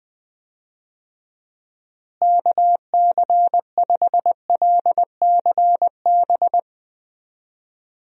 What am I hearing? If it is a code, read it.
Morse code, heard as a keyed tone, about 20 wpm, 700 Hz. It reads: KC5LCB
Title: USA Call Signs - 6 Characters - Repeated 2x - 20wpm